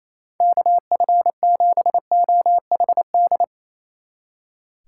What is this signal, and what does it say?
Morse code, heard as a keyed tone, about 36 wpm, 700 Hz. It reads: KF7OHD